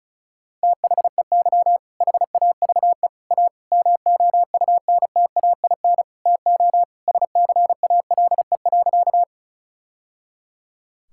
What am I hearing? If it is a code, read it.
Morse code, heard as a keyed tone, about 36 wpm, 700 Hz. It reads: THEY HAVE A MOUNTAIN TO SCALE.
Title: Sentences from Top 1000 Words - Review Part 5 - 35wpm